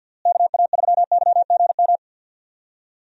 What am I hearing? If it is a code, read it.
Morse code, heard as a keyed tone, about 50 wpm, 700 Hz. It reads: KA3YGM
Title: USA Call Signs - 6 Characters - Repeated 3x - 50wpm